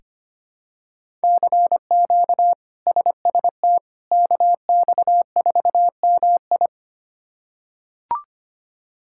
Morse code, heard as a keyed tone, about 25 wpm, 700 Hz. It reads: CQ SST KX4MI
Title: K1USN SST QSO Practice - 25wpm